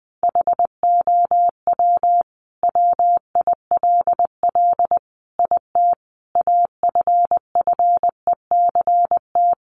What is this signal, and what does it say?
Morse code, heard as a keyed tone, about 20 wpm, 700 Hz. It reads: HOW WILL IT AFFECT
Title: Sentences from Top 900 Words 20wpm